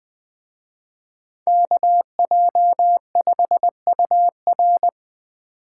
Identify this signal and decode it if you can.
Morse code, heard as a keyed tone, about 25 wpm, 700 Hz. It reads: KJ5UR
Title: Speed Racing - USA 2x2 Call Signs 20wpm